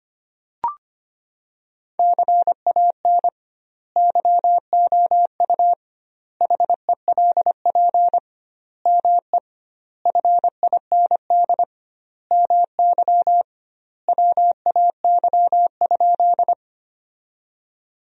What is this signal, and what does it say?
Morse code, heard as a keyed tone, about 25 wpm, 700 Hz. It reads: CAN YOU HELP ME FIND MY WAY?
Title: 2x Word-Spacing - Sentences from Top 200 Words - 25wpm